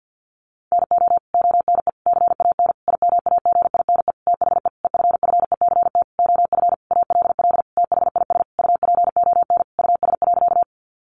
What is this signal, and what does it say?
Morse code, heard as a keyed tone, about 50 wpm, 700 Hz. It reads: NO ONE CAN IMAGINE THE EFFECT OF ALL THIS UPON US.
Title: Sentences from Top 1400 Words 50wpm